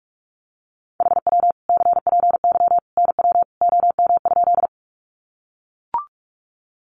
Morse code, heard as a keyed tone, about 44 wpm, 700 Hz. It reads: HW CPY NW OM?
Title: Common QSO Phrases 45wpm